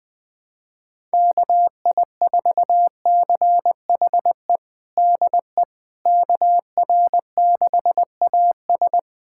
Morse code, heard as a keyed tone, about 21 wpm, 700 Hz. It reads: KI4CHE DE KR6AS
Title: USA Callsign DE Callsign 20wpm